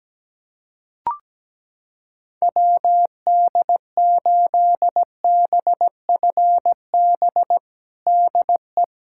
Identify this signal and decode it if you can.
Morse code, heard as a keyed tone, about 17 wpm, 700 Hz. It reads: WD8BFB DE
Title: USA Callsign DE Callsign - 6 Characters - 17wpm